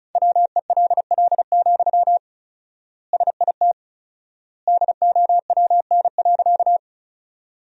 Morse code, heard as a keyed tone, about 36 wpm, 700 Hz. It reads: WELL, SIT DOWN.
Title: 4x Word-Spacing - Sentences from Top 500 Words - 35wpm